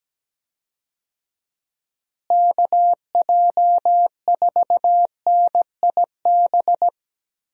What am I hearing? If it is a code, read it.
Morse code, heard as a keyed tone, about 17 wpm, 700 Hz. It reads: KJ4NIB
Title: USA Call Signs - 6 Characters - Repeated 3x - 17wpm